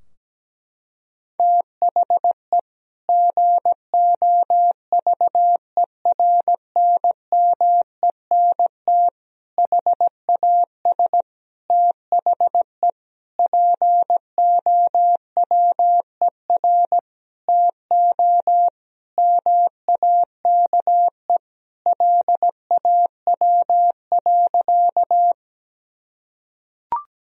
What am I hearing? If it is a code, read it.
Morse code, heard as a keyed tone, about 17 wpm, 700 Hz. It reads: THE GOVERNMENT HAS THE POWER TO MAKE LAW.
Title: Sentences from Top 1000 Words - Review Part 3 - 17wpm